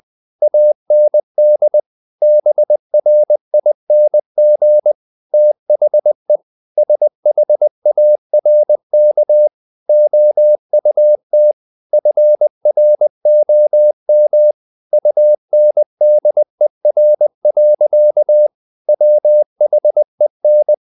Morse code, heard as a keyed tone, about 20 wpm, 600 Hz. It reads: AND BRING THE SHARK OUT FROM UNDER. WHEN
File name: the old man and the sea part 2 20wpm